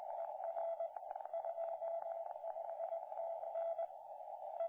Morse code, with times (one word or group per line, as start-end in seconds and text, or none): none